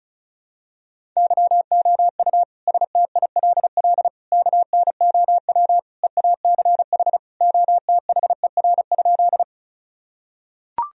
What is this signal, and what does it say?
Morse code, heard as a keyed tone, about 36 wpm, 700 Hz. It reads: YOU STILL KNOW EACH OTHER?
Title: Sentences from Top 1000 Words - Review Part 1 - 35wpm